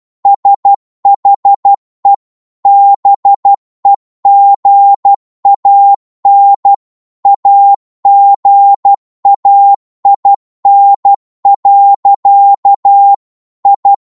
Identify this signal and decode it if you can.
Morse code, heard as a keyed tone, about 12 wpm, 800 Hz. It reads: SHE BEGAN AGAIN. I
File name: Morse / psg-12wpm